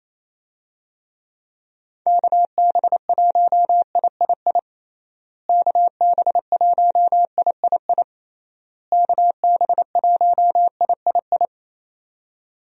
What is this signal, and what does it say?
Morse code, heard as a keyed tone, about 28 wpm, 700 Hz. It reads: KB1III KB1III KB1III
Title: USA Call Signs - 6 Characters - Repeated 3x - 28wpm